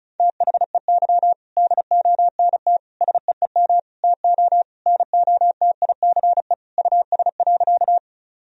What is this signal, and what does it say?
Morse code, heard as a keyed tone, about 34 wpm, 700 Hz. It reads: THEY DONT SEEM TO NOTICE US.
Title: Sentences from Top 1000 Words - Review Part 2 - 35wpm